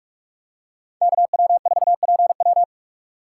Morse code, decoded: KW4PW

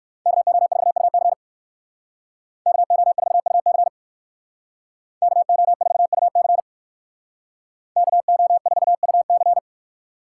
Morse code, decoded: KO4UC KO4UC KO4UC KO4UC

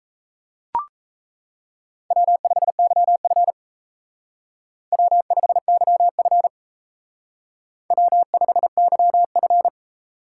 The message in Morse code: W5YF W5YF W5YF

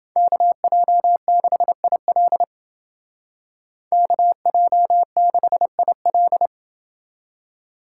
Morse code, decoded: KJ6IL KJ6IL

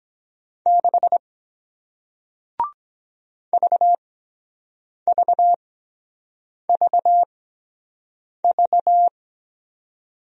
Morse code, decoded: 6 VVVV

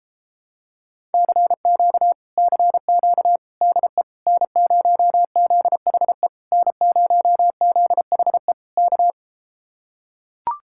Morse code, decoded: CQ CQ DE N0ZHE N0ZHE K

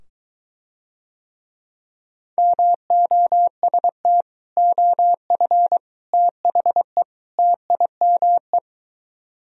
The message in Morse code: MOST OF THE TIME

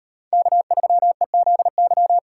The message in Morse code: K3EZY

